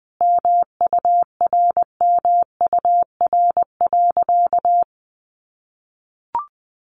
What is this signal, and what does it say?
Morse code, heard as a keyed tone, about 20 wpm, 700 Hz. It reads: MURMUR.
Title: A Sound Of Thunder 20wpm